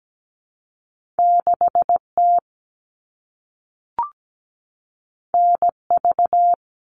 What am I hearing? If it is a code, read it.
Morse code, heard as a keyed tone, about 17 wpm, 700 Hz. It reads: 6T NV
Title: All Cut Numbers - 2-Digit 17wpm